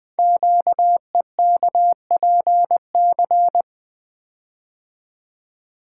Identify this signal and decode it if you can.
Morse code, heard as a keyed tone, about 20 wpm, 700 Hz. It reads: QEKPC
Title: Code Groups - 5 Letters - 20wpm